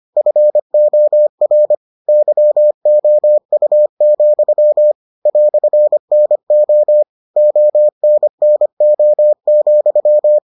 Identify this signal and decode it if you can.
Morse code, heard as a keyed tone, about 25 wpm, 600 Hz. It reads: FOR YOU, "NO ONNO,